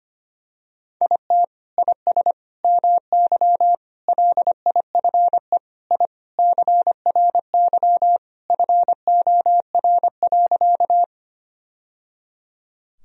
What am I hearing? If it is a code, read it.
Morse code, heard as a keyed tone, about 25 wpm, 700 Hz. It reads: IT IS MY LIFE I CRY FOR.
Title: Sentences from Top 1000 Words - Review Part 2 - 25wpm